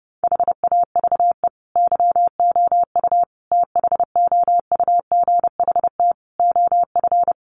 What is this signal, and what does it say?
Morse code, heard as a keyed tone, about 30 wpm, 700 Hz. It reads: HAVE YOU THOUGHT OF
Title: Sentences from Top 700 Words 30wpm